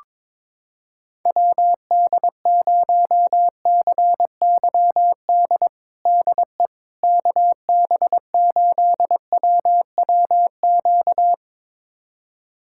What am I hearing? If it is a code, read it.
Morse code, heard as a keyed tone, about 22 wpm, 700 Hz. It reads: WD0CYD DE KB8WWQ
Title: USA Callsign DE Callsign - 6 Characters - 22wpm